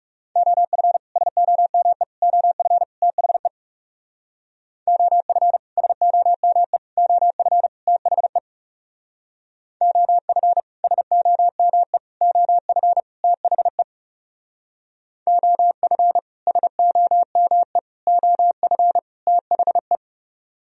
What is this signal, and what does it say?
Morse code, heard as a keyed tone, about 37 wpm, 700 Hz. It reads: OF SOME OF THE OF SOME OF THE OF SOME OF THE OF SOME OF THE
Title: Speed-Racing - Sets of 4 Words 30wpm